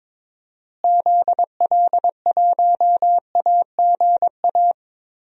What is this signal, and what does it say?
Morse code, heard as a keyed tone, about 22 wpm, 700 Hz. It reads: ZL1AGA
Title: DXCC-100 Call Signs - 6 Characters - Repeated 3x - 22wpm